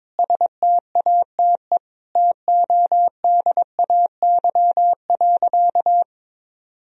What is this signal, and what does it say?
Morse code, heard as a keyed tone, about 22 wpm, 700 Hz. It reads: STATE TODAY.
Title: Sentences from Top 1000 Words - Review Part 2 - 22wpm